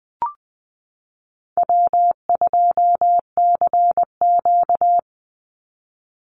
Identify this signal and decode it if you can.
Morse code, heard as a keyed tone, about 20 wpm, 700 Hz. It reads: W2CQ
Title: Easy Call Signs Repeated 2x - 20wpm